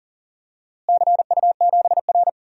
Morse code, heard as a keyed tone, about 39 wpm, 700 Hz. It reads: CU7R
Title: DXCC-100 Call Signs - 4 Characters - Repeated 2x - 40wpm